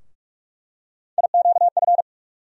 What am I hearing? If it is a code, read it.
Morse code, heard as a keyed tone, about 44 wpm, 700 Hz. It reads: IQF
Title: USA Call Sign Postfixes - 3 Letters 45wpm